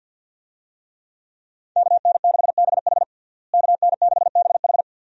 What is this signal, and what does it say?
Morse code, heard as a keyed tone, about 52 wpm, 700 Hz. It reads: KN6BH KN6BH
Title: USA Call Signs - 5 Characters - Repeated 2x - 50wpm